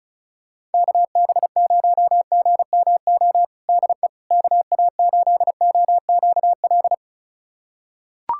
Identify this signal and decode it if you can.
Morse code, heard as a keyed tone, about 36 wpm, 700 Hz. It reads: KB0GMO DE KA8OQL
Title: USA Callsign DE Callsign 35wpm